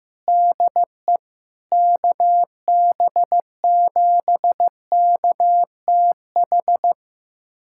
Morse code, heard as a keyed tone, about 15 wpm, 700 Hz. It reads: DE KB7KTH
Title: USA Callsign DE Callsign 15wpm